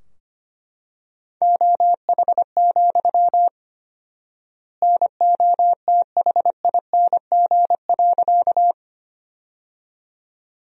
Morse code, heard as a keyed tone, about 25 wpm, 700 Hz. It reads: OH, NOTHING.
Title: 4x Word-Spacing - Sentences from Top 500 Words - 25wpm